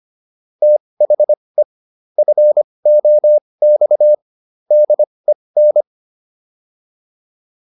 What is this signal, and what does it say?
Morse code, heard as a keyed tone, about 25 wpm, spaced as for 20 wpm, 600 Hz. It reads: THE FOX DEN